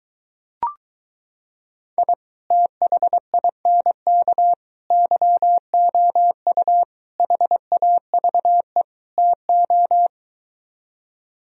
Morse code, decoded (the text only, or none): I THINK YOU HAVE TO